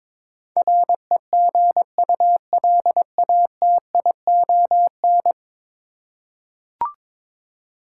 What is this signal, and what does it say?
Morse code, heard as a keyed tone, about 21 wpm, 700 Hz. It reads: REGULATION
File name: practice / 22wpm / 10 Letter Words - First 3 Letters Given - 22wpm